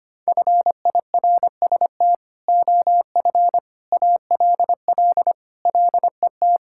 Morse code, heard as a keyed tone, about 25 wpm, 700 Hz. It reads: FIRST OF ALL LET
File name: Sentences with 3-Word Phrases - 25wpm